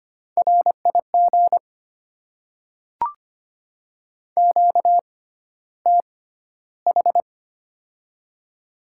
Morse code, heard as a keyed tone, about 25 wpm, spaced as for 13 wpm, 700 Hz. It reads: RIG QTH
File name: Top 20 QSO Elements - 6x Character Spacing - 25wpm